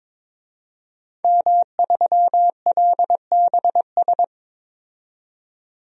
Morse code, heard as a keyed tone, about 21 wpm, 700 Hz. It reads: M3LBS
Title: DXCC-100 Call Signs - 5 Characters - 22wpm